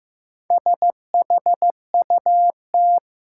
Morse code, heard as a keyed tone, about 16 wpm, 700 Hz. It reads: SHUT
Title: Four Letter Words - 15wpm